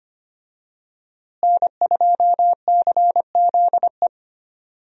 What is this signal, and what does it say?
Morse code, heard as a keyed tone, about 25 wpm, 700 Hz. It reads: N2CZE
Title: USA Call Signs - 5 Characters - Repeated 2x - 25wpm